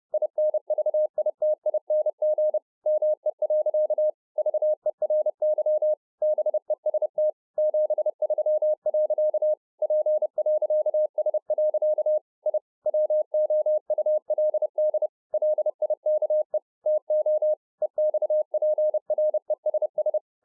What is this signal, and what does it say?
Morse code, heard as a keyed tone, about 30 wpm, 610 Hz. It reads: INVITING ME. VERY BEST 73. P.S. I WOULD LIKE TO EXPRESS